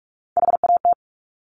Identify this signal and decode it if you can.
Morse code, heard as a keyed tone, about 46 wpm, 700 Hz. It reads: HAT
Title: Top Three Letter Combinations in Words 45wpm